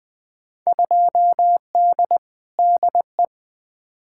Morse code, heard as a keyed tone, about 20 wpm, 700 Hz. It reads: 2D DE
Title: Field Day Exchanges Encore - 20wpm